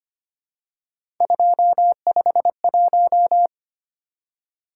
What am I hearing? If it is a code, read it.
Morse code, heard as a keyed tone, about 25 wpm, 700 Hz. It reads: 251
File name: Three Letter-Number Combinations - 25wpm